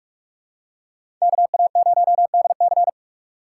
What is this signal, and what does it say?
Morse code, heard as a keyed tone, about 42 wpm, 700 Hz. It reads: KA0DC
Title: USA Call Signs - 2x Character Spacing - 45wpm